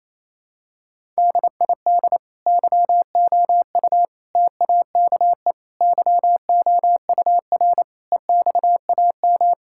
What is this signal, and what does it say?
Morse code, heard as a keyed tone, about 28 wpm, 700 Hz. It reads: DID YOU TAKE YOUR EXAM
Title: Sentences from Top 1000 Words - Review Part 5 - 28wpm